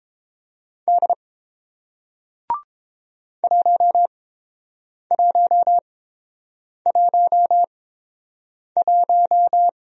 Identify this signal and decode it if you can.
Morse code, heard as a keyed tone, about 30 wpm, spaced as for 7 wpm, 700 Hz. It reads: D 1111